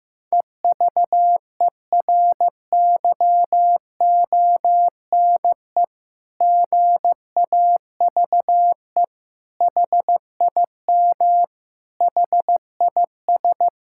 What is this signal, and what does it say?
Morse code, heard as a keyed tone, about 15 wpm, 700 Hz. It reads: EVERYONE GAVE HIM HIS
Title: Sentences from Top 1000 Words - Review Part 6 - 15wpm